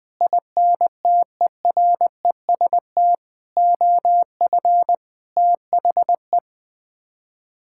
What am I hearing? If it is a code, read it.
Morse code, heard as a keyed tone, about 20 wpm, 700 Hz. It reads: INTEREST OF THE
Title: Sets of 6 Words - 20wpm